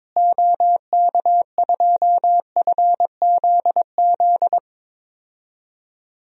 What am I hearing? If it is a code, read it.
Morse code, heard as a keyed tone, about 22 wpm, 700 Hz. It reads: OK2FZZ
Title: DXCC-100 Call Signs - 6 Characters - Repeated 3x - 22wpm